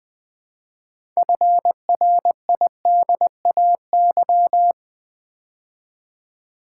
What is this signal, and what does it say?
Morse code, heard as a keyed tone, about 20 wpm, 700 Hz. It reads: FRIDAY
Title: Build A Word - 3 Letters At A Time - 6-Letter Words - 20wpm